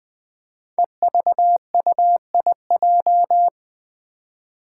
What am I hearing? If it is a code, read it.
Morse code, heard as a keyed tone, about 20 wpm, 700 Hz. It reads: EVUIJ
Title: Five Challenging Letter-Number Combinations 20wpm